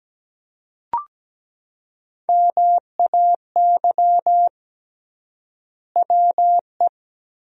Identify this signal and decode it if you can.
Morse code, heard as a keyed tone, about 17 wpm, 700 Hz. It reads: MAY WE
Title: 3x Word-Spacing - Sentences from Top 100 Words - 17wpm